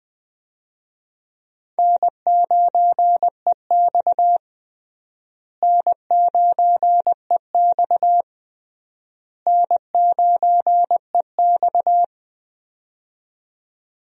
Morse code, heard as a keyed tone, about 21 wpm, 700 Hz. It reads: N9EX N9EX N9EX
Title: USA Call Signs - 4 Characters - Repeated 3x - 20wpm